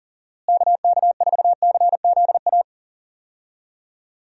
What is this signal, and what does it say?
Morse code, heard as a keyed tone, about 40 wpm, 700 Hz. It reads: KK4CZA
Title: USA Call Signs - 6 Characters - Repeated 3x - 40wpm